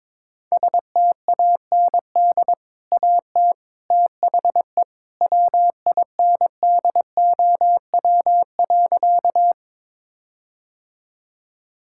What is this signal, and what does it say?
Morse code, heard as a keyed tone, about 22 wpm, 700 Hz. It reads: STAND AT THE WINDOW.
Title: Sentences from Top 1600 Words 22wpm